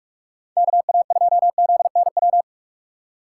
Morse code, heard as a keyed tone, about 46 wpm, 700 Hz. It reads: KA2ZNW
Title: USA Call Signs - 6 Characters - 6x Character Spacing - 45wpm